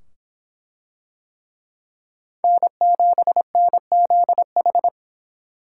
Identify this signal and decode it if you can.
Morse code, heard as a keyed tone, about 24 wpm, 700 Hz. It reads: N7NZH